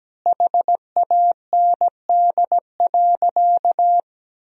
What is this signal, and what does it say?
Morse code, heard as a keyed tone, about 17 wpm, 700 Hz. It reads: HAND.
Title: 4x Word-Spacing - Sentences from Top 400 Words - 17wpm